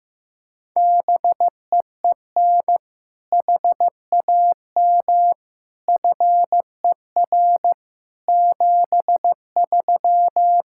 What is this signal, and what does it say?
Morse code, heard as a keyed tone, about 15 wpm, 700 Hz. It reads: BEEN HAM FER 73
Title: Practice QSO - Exchange 3 - More Info - 15wpm